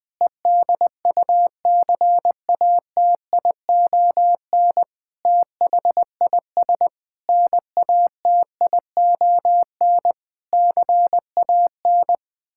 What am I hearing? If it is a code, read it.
Morse code, heard as a keyed tone, about 21 wpm, 700 Hz. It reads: EDUCATION THIS NATION CAN
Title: Sentences from Top 1000 Words - Review Part 6 - 20wpm